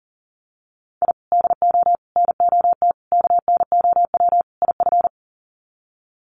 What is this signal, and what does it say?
Morse code, heard as a keyed tone, about 40 wpm, 700 Hz. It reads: I DO NOT KNOW IF
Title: Sets of 5 Words 40wpm